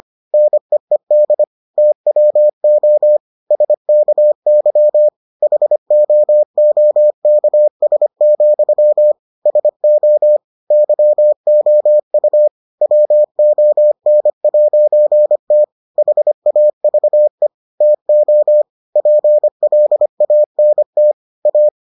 Morse code, heard as a keyed tone, about 25 wpm, 600 Hz. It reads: NEED TWO SKY HOOKS, SO YOU WON'T HAVE TO PLANT A